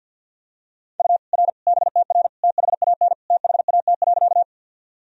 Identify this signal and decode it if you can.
Morse code, heard as a keyed tone, about 52 wpm, 700 Hz. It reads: U R BTR THAN THAT.
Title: Sentences from Top 400 Words and Abbreviations 50wpm